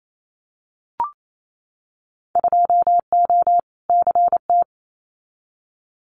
Morse code, heard as a keyed tone, about 28 wpm, 700 Hz. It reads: 2O CT
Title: Winter Field Day Exchanges 28wpm